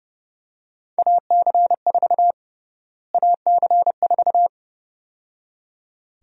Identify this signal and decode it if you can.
Morse code, heard as a keyed tone, about 30 wpm, 700 Hz. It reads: AC4 AC4
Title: USA Call Sign Prefixes - Repeated 2x - 30wpm